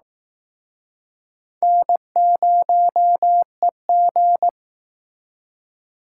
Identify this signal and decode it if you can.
Morse code, heard as a keyed tone, about 19 wpm, 700 Hz. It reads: N0EG
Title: Speed Racing - USA 1x2 Call Signs 15wpm